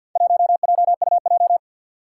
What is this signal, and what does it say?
Morse code, heard as a keyed tone, about 47 wpm, 700 Hz. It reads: 1JUJ
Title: Four Challenging Letter-Number Combinations 50wpm